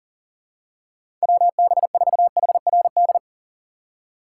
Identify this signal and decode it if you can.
Morse code, heard as a keyed tone, about 40 wpm, 700 Hz. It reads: WB4HRD
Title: USA Call Signs - 6 Characters - 4x Character Spacing - 40wpm